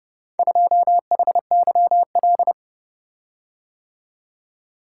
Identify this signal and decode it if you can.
Morse code, heard as a keyed tone, about 29 wpm, 700 Hz. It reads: 2HYL